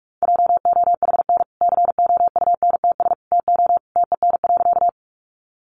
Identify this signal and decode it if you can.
Morse code, heard as a keyed tone, about 44 wpm, 700 Hz. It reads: JOHN COUNTS TO TEN.
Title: Easy Phrases 45wpm